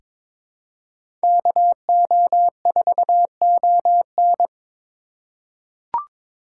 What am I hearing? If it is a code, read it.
Morse code, heard as a keyed tone, about 22 wpm, 700 Hz. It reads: KO4ON
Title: USA Call Signs - 3x Character Spacing - 22wpm